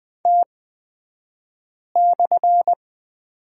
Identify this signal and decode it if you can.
Morse code, heard as a keyed tone, about 20 wpm, spaced as for 5 wpm, 700 Hz. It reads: T/
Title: 254 Lesson 41d - Period - Two Characters 20wpm